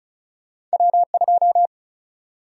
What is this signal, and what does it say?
Morse code, heard as a keyed tone, about 36 wpm, 700 Hz. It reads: W2
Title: USA Call Sign Prefixes - 35wpm